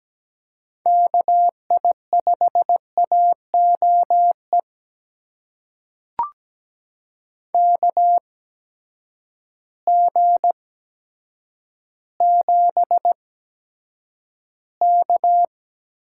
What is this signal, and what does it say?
Morse code, heard as a keyed tone, about 17 wpm, spaced as for 8 wpm, 700 Hz. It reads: KI5AOE KG7K
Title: USA Call Signs - 8x Character Spacing - 17wpm